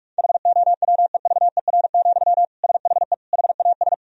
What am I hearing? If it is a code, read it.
Morse code, heard as a keyed tone, about 46 wpm, 700 Hz. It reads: HOWEVER, SHE HAS
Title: Sentences from Top 1000 Words - Review Part 2 - 45wpm